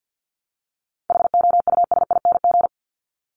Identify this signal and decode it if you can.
Morse code, heard as a keyed tone, about 50 wpm, 700 Hz. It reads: HOUSING
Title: Top 1001-1250 Words 50wpm